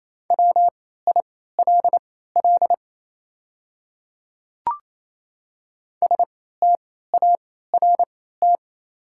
Morse code, heard as a keyed tone, about 27 wpm, spaced as for 16 wpm, 700 Hz. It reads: WILL START
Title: 3x Character Spacing - Top 300 Words - 28wpm